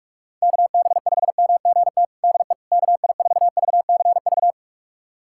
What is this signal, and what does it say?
Morse code, heard as a keyed tone, about 46 wpm, 700 Hz. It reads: KB5MGT DE KI4VCV
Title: USA Callsign DE Callsign - 6 Characters - 45wpm